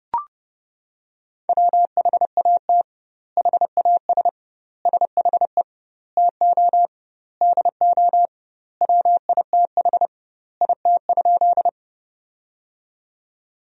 Morse code, heard as a keyed tone, about 32 wpm, 700 Hz. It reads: WHAT HAS SHE TO DO WITH IT?
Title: 2x Word-Spacing - Sentences from Top 100 Words - 30wpm